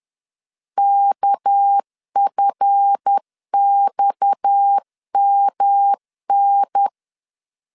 Morse code, heard as a keyed tone, about 10 wpm, 800 Hz. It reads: KFXMN
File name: LEZIONI / 7 GF9Z / 7 GF9 10wpm 2747